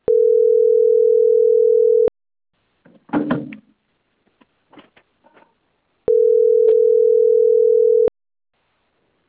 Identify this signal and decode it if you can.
Ringback tone (North America), 440 + 480 Hz.